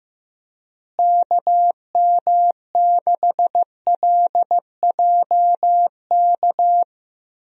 Morse code, heard as a keyed tone, about 15 wpm, 700 Hz. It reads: KM6LJK
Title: USA Call Signs - 6 Characters - Repeated 2x - 15wpm